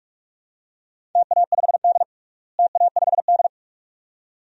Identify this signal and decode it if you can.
Morse code, heard as a keyed tone, about 46 wpm, 700 Hz. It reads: TA5D TA5D
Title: DXCC-100 Call Signs - 4 Characters - Repeated 2x - 45wpm